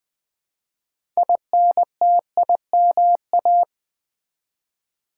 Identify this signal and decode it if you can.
Morse code, heard as a keyed tone, about 20 wpm, 700 Hz. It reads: INTIMA
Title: Build A Word - 2 Letters At A Time - 8-Letter Words - 20wpm